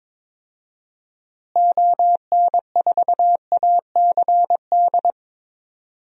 Morse code, heard as a keyed tone, about 22 wpm, 700 Hz. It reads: ON4ACD
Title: DXCC-100 Call Signs - 6 Characters - Repeated 2x - 22wpm